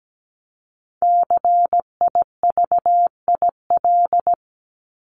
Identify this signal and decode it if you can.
Morse code, heard as a keyed tone, about 17 wpm, 700 Hz. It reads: CIVIL
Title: Top 1751-2000 Words 17wpm